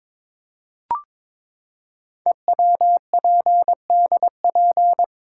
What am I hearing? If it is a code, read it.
Morse code, heard as a keyed tone, about 22 wpm, 700 Hz. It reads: EWPDP
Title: Code Groups - 5 Letters - 22wpm